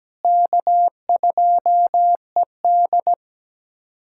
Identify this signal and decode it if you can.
Morse code, heard as a keyed tone, about 17 wpm, 700 Hz. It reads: K2ED
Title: USA Callsign DE Callsign - 4 Characters - 17wpm